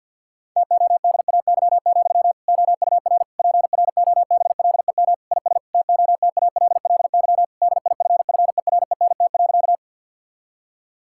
Morse code, heard as a keyed tone, about 50 wpm, 700 Hz. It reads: TODAY, OUR PROBLEM IS TOTALLY DIFFERENT.
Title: Sentences from Top 1600 Words 50wpm